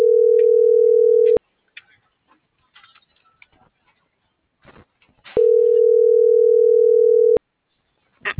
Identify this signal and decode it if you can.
Ringback tone (North America), 440 + 480 Hz.